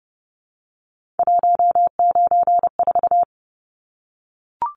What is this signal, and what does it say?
Morse code, heard as a keyed tone, about 30 wpm, 700 Hz. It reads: 194
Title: Numbers - 3-Digit 30wpm